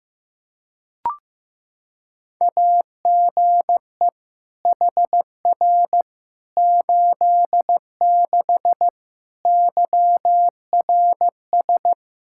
Morse code, heard as a keyed tone, about 15 wpm, 700 Hz. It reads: AGE HR 86 YRS
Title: Practice QSO - Exchange 3 - More Info - 15wpm